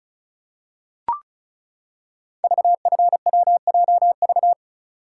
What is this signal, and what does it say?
Morse code, heard as a keyed tone, about 36 wpm, 700 Hz. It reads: VFWJV
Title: Code Groups - 5 Letters - 35wpm